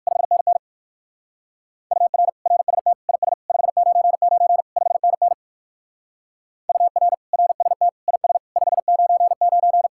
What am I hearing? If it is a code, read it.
Morse code, heard as a keyed tone, about 58 wpm, 700 Hz. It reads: 5NN UR RST IS 599 5NN UR RST IS 599